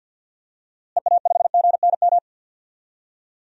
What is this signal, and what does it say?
Morse code, heard as a keyed tone, about 51 wpm, 700 Hz. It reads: EA5GNM
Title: DXCC-100 Call Signs - 6 Characters - Repeated 2x - 50wpm